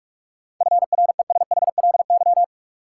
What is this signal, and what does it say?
Morse code, heard as a keyed tone, about 46 wpm, 700 Hz. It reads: FRESHLY
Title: Sets of 5 Words Encore - New Word Drill - 45wpm